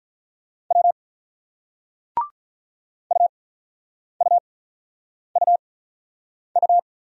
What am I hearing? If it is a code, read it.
Morse code, heard as a keyed tone, about 46 wpm, spaced as for 9 wpm, 700 Hz. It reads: W UUUU